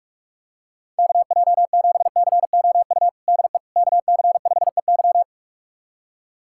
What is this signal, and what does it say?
Morse code, heard as a keyed tone, about 42 wpm, 700 Hz. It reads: KJ7COU DE KC5EY
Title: USA Callsign DE Callsign 45wpm